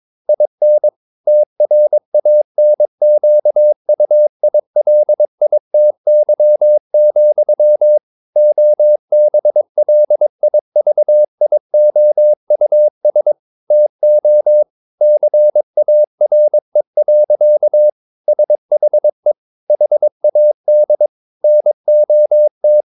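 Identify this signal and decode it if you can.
Morse code, heard as a keyed tone, about 22 wpm, 600 Hz. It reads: IN TRANQUILITY, OBLIVIOUS TO CARE. SHE HAD NOT
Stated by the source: Morse code practice at 22 wpm, sending text from an English novel.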